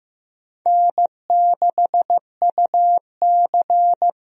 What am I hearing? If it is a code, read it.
Morse code, heard as a keyed tone, about 15 wpm, 700 Hz. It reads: N6UC
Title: USA Call Signs - 4 Characters - Repeated 3x - 15wpm